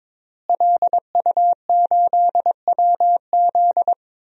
Morse code, heard as a keyed tone, about 22 wpm, 700 Hz. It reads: LU8WZ
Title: DXCC-100 Call Signs - 5 Characters - Repeated 3x - 22wpm